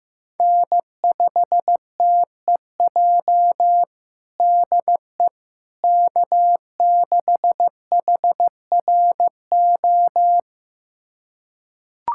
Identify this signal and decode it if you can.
Morse code, heard as a keyed tone, about 15 wpm, 700 Hz. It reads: N5TEJ DE K6HRO
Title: USA Callsign DE Callsign 15wpm